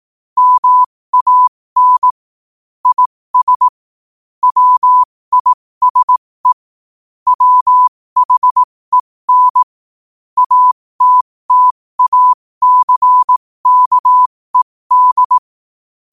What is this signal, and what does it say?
Morse code, heard as a keyed tone, about 18 wpm, spaced as for 15 wpm, 1000 Hz. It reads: MAN IS WISE WHEN ATTACKED
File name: qotdfiles / 15wpm